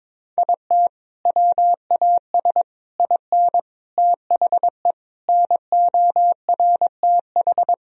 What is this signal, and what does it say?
Morse code, heard as a keyed tone, about 22 wpm, 700 Hz. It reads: IT WAS IN THE NORTH
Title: Sentences from Top 1000 Words - Review Part 1 - 22wpm